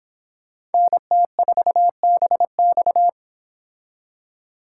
Morse code, heard as a keyed tone, about 31 wpm, 700 Hz. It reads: NT4BX